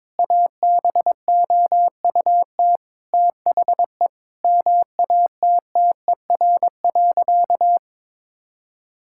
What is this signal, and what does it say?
Morse code, heard as a keyed tone, about 21 wpm, 700 Hz. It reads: ABOUT THE MATTER.
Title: Sentences from Top 1000 Words - Review Part 5 - 22wpm